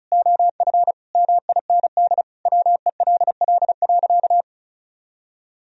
Morse code, heard as a keyed tone, about 36 wpm, 700 Hz. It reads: OF MIND WELL.
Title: Sentences from Top 1000 Words - Review Part 2 - 35wpm